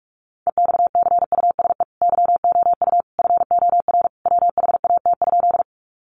Morse code, heard as a keyed tone, about 44 wpm, 700 Hz. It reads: EXCUSE YOU FOR WHAT?
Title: Sentences from Top 700 Words 45wpm